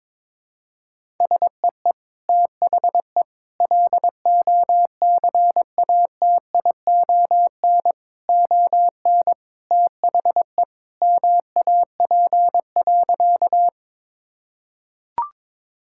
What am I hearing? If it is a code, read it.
Morse code, heard as a keyed tone, about 21 wpm, 700 Hz. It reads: SEE THE LOCATION ON THE MAP.
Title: Sentences from Top 1000 Words - Review Part 6 - 22wpm